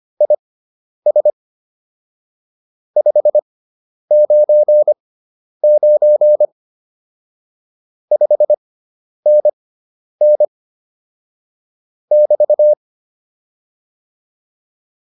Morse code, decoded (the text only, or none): IS 599 5NN <BT>